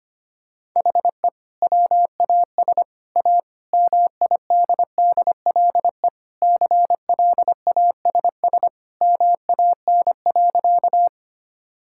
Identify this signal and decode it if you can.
Morse code, heard as a keyed tone, about 25 wpm, 700 Hz. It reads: HE WAS A MIDDLE CLASS MAN.
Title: Sentences from Top 1000 Words - Review Part 6 - 25wpm